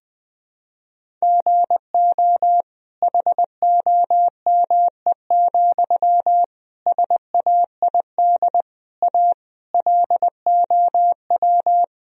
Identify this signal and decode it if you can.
Morse code, heard as a keyed tone, about 20 wpm, 700 Hz. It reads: GO HOME, SAID A LOW
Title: Sentences from Top 1000 Words - Review Part 1 - 20wpm